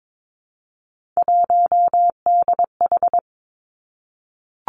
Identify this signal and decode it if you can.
Morse code, heard as a keyed tone, about 22 wpm, 700 Hz. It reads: 1DH
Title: Three Challenging Letter-Number Combinations 22wpm